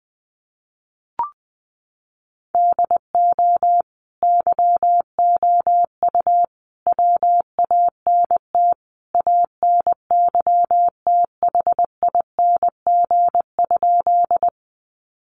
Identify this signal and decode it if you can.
Morse code, heard as a keyed tone, about 20 wpm, 700 Hz. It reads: DO YOU WANT ANYTHING?
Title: Sentences from Top 1000 Words 20wpm